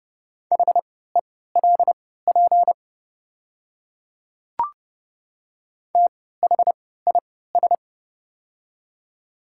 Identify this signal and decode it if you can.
Morse code, heard as a keyed tone, about 30 wpm, spaced as for 17 wpm, 700 Hz. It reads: HELP THIS
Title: 3x Character Spacing - Top 500 Words - 30wpm